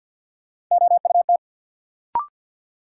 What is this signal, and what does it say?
Morse code, heard as a keyed tone, about 53 wpm, 700 Hz. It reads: OUT
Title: Sets of 2 Words - New Word Drill - 50wpm